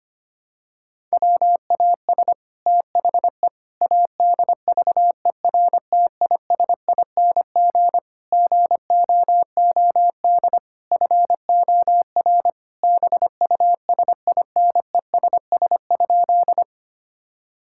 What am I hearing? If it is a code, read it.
Morse code, heard as a keyed tone, about 25 wpm, 700 Hz. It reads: WAS THE ADVERTISING GOOD FOR BUSINESS?
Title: Sentences from Top 1000 Words - Review Part 6 - 25wpm